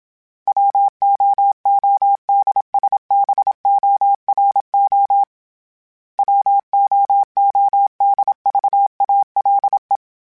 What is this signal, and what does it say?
Morse code, heard as a keyed tone, about 27 wpm, 800 Hz. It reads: WOODSBORO WOODVALE
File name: fromSMs / FromSMs mp3s / Cities@25